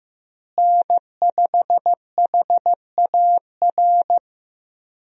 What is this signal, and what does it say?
Morse code, heard as a keyed tone, about 15 wpm, 700 Hz. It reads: N5HAR
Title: USA Call Signs - 5 Characters - Repeated 3x - 15wpm